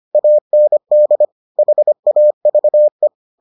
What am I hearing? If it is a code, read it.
Morse code, heard as a keyed tone, about 25 wpm, 600 Hz. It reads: AND HAVE